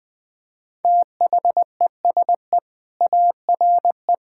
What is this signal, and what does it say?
Morse code, heard as a keyed tone, about 20 wpm, 700 Hz. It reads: THESE ARE
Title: Sentences with 4-Word Phrases - Spoken Beginning - 20wpm